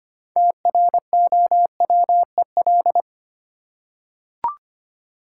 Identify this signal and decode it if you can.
Morse code, heard as a keyed tone, about 25 wpm, 700 Hz. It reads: TROWEL
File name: practice / 25wpm / Easy Words with Hints - 25wpm